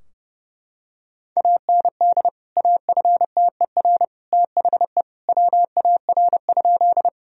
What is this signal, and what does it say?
Morse code, heard as a keyed tone, about 32 wpm, 700 Hz. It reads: AND AFTER THE WAR?
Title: Sentences from Top 1000 Words - Review Part 2 - 30wpm